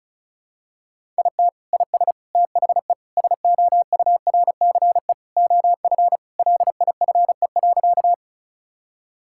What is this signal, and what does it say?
Morse code, heard as a keyed tone, about 36 wpm, 700 Hz. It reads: IT IS THE SOURCE OF LIFE.
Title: Sentences from Top 1000 Words - Review Part 5 - 35wpm